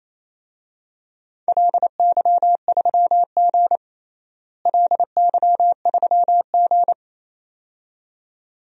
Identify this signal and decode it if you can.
Morse code, heard as a keyed tone, about 28 wpm, 700 Hz. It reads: LY3G LY3G
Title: DXCC-100 Call Signs - 4 Characters - Repeated 2x - 28wpm